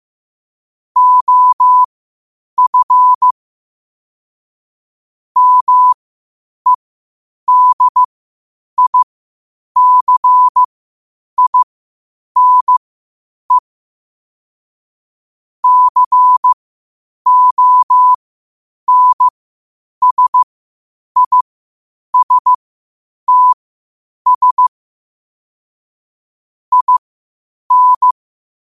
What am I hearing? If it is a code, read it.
Morse code, heard as a keyed tone, about 15 wpm, spaced as for 8 wpm, 1000 Hz. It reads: OF MEDICINE CONSISTS IN
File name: qotdfiles / 8wpm